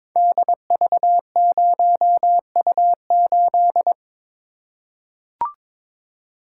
Morse code, heard as a keyed tone, about 22 wpm, 700 Hz. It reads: DV0U8
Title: Code Groups - 5 Letter:Numbers - 22wpm